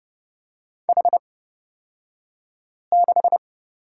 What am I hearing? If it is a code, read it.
Morse code, heard as a keyed tone, about 29 wpm, spaced as for 5 wpm, 700 Hz. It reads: H6
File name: Lesson 36d - BT - Two Characters 30wpm